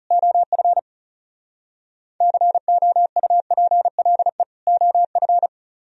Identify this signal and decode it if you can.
Morse code, heard as a keyed tone, about 34 wpm, 700 Hz. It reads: OF COUPLE OF
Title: Speed-Racing - Sets of 2 Words 30wpm